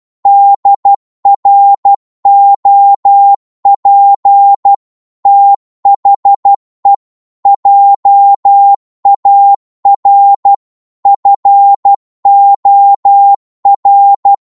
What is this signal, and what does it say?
Morse code, heard as a keyed tone, about 12 wpm, 800 Hz. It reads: DROP THE JAR FOR
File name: cwmp3 / psg-1-12wpm